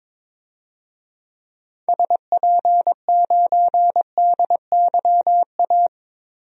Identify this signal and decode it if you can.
Morse code, heard as a keyed tone, about 21 wpm, 700 Hz. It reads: SP9DYA
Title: DXCC-100 Call Signs - 6 Characters - 22wpm